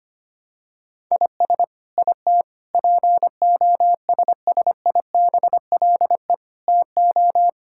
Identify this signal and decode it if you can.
Morse code, heard as a keyed tone, about 25 wpm, 700 Hz. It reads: IS IT POSSIBLE TO
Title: Sentences from Top 1000 Words - Review Part 6 - 25wpm